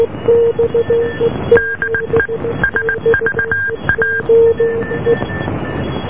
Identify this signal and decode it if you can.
Morse code, heard as a keyed tone, about 19 wpm, 1600 Hz, with noise. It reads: DE SVA